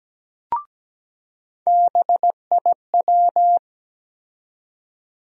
Morse code, heard as a keyed tone, about 17 wpm, 700 Hz. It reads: BIW
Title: USA Call Sign Postfixes - 3 Letters 17wpm